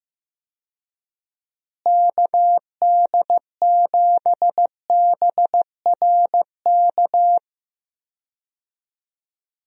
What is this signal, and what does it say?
Morse code, heard as a keyed tone, about 16 wpm, 700 Hz. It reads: KD7BRK
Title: USA Call Signs - 6 Characters - Repeated 3x - 15wpm